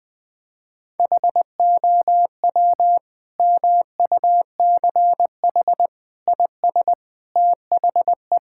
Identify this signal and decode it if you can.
Morse code, heard as a keyed tone, about 20 wpm, 700 Hz. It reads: HOW MUCH IS THE
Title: Sentences from Top 1000 Words - Review Part 5 - 20wpm